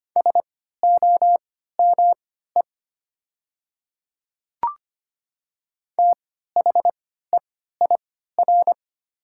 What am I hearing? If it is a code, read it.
Morse code, heard as a keyed tone, about 25 wpm, spaced as for 14 wpm, 700 Hz. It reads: SOME THEIR
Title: 3x Character Spacing - Top 100 Words - 25wpm